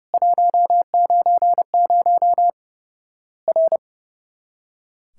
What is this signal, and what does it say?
Morse code, heard as a keyed tone, about 29 wpm, 700 Hz. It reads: 190 R
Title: Worked All Europe DX Contest - QTC Exchange - 30wpm